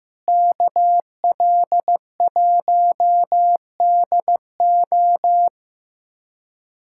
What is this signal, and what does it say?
Morse code, heard as a keyed tone, about 15 wpm, 700 Hz. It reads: KL1DO
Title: USA Callsign DE Callsign - 5 Characters - 15wpm